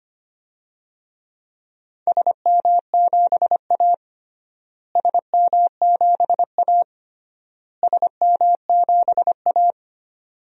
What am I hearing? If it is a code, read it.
Morse code, heard as a keyed tone, about 25 wpm, 700 Hz. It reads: SM7A SM7A SM7A
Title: DXCC-100 Call Signs - 4 Characters - Repeated 3x - 25wpm